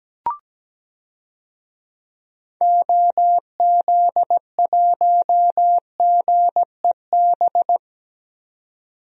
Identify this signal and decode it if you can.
Morse code, heard as a keyed tone, about 17 wpm, 700 Hz. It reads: OZ1GEB
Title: DXCC-100 Call Signs - 6 Characters - 17wpm